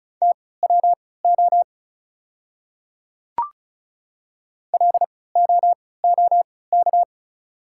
Morse code, heard as a keyed tone, about 36 wpm, spaced as for 20 wpm, 690 Hz. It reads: TWO LOOK
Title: 3x Character Spacing - Top 200 Words - 35wpm